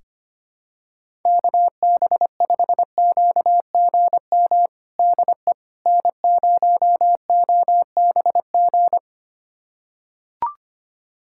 Morse code, heard as a keyed tone, about 25 wpm, 700 Hz. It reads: KB5QGM DE N0OBG
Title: USA Callsign DE Callsign 25wpm